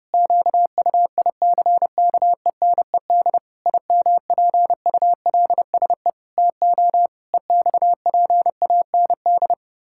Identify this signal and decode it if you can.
Morse code, heard as a keyed tone, about 30 wpm, 700 Hz. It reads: QUICKENED IMPULSE TO EXPAND
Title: Pangram Challenge v2 - 30wpm